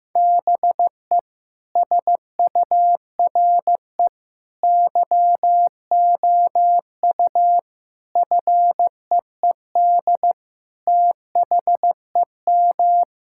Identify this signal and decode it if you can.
Morse code, heard as a keyed tone, about 16 wpm, 700 Hz. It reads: BE SURE YOU FEED THEM
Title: Sentences from Top 1600 Words 15wpm